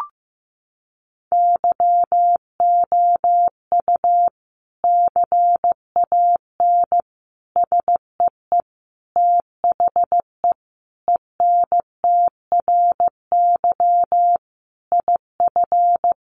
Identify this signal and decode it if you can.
Morse code, heard as a keyed tone, about 15 wpm, 700 Hz. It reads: YOU CAN SEE THE ENTRY IF
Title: Sentences from Top 1500 Words 15wpm